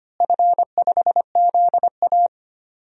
Morse code, heard as a keyed tone, about 25 wpm, 700 Hz. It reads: F5ZA
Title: DXCC-100 Call Signs - 4 Characters - Repeated 3x - 25wpm